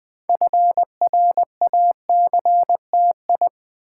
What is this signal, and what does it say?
Morse code, heard as a keyed tone, about 20 wpm, 700 Hz. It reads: FRACTI
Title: Build A Word - 3 Letters At A Time - 8-Letter Words - 20wpm